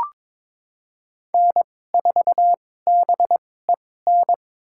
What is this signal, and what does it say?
Morse code, heard as a keyed tone, about 22 wpm, spaced as for 16 wpm, 700 Hz. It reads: N4BEN
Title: USA Call Signs - 5 Characters - 2x Character Spacing - 22wpm